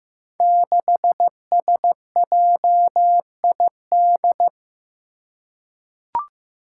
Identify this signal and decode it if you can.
Morse code, heard as a keyed tone, about 15 wpm, 700 Hz. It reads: 6SJID
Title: Five Challenging Letter-Number Combinations 15wpm